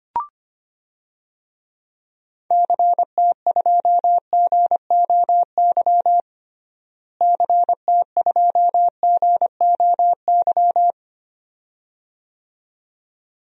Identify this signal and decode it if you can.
Morse code, heard as a keyed tone, about 25 wpm, 700 Hz. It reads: CT2GOY CT2GOY
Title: DXCC-100 Call Signs - 6 Characters - Repeated 2x - 25wpm